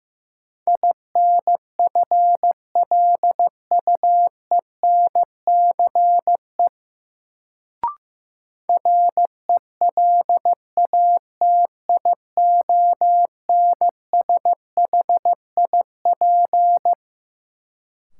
Morse code, heard as a keyed tone, about 15 wpm, 700 Hz. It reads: INFLUENCE RELATIONSHIP
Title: Sentences from Top 1100 Words - New Word Drill - 15wpm